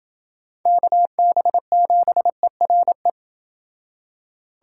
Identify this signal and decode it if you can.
Morse code, heard as a keyed tone, about 31 wpm, 700 Hz. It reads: KB7ERE